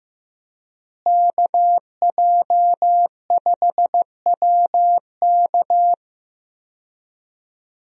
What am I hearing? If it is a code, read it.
Morse code, heard as a keyed tone, about 15 wpm, 700 Hz. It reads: KJ5WK
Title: USA Call Signs - 5 Characters - Repeated 2x - 15wpm